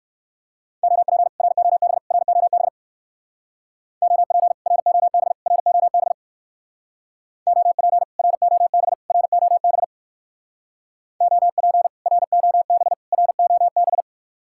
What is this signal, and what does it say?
Morse code, heard as a keyed tone, about 56 wpm, 700 Hz. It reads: OP ROB ROB OP ROB ROB OP ROB ROB OP ROB ROB
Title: Speed-Racing Common QSO Phrases Encore - 45wpm